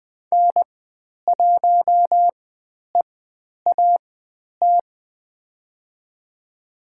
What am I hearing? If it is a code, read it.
Morse code, heard as a keyed tone, about 20 wpm, spaced as for 10 wpm, 700 Hz. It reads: N1EAT
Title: 095 Lesson 16g - Letter c - Call Signs 20wpm